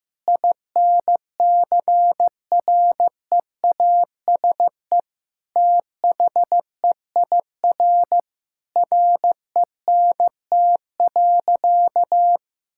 Morse code, heard as a keyed tone, about 15 wpm, 700 Hz. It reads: INCREASE THEIR RENT.
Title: Sentences from Top 1000 Words - Review Part 5 - 15wpm